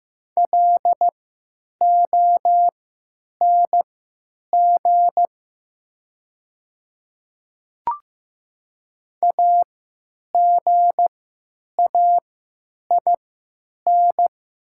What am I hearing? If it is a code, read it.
Morse code, heard as a keyed tone, about 15 wpm, spaced as for 9 wpm, 700 Hz. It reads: LONG AGAIN
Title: 3x Character Spacing - Top 200 Words - 15wpm